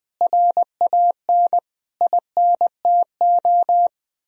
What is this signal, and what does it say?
Morse code, heard as a keyed tone, about 19 wpm, 700 Hz. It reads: RAN INTO
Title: Sentences from Top 1000 Words - Review Part 3 - 20wpm